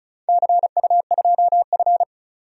Morse code, heard as a keyed tone, about 36 wpm, 700 Hz. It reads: CU2F
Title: DXCC-100 Call Signs - 4 Characters - 35wpm